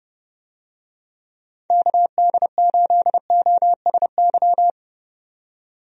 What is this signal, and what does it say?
Morse code, heard as a keyed tone, about 31 wpm, 700 Hz. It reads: KD8OSY
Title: USA Call Signs - 6 Characters - 30wpm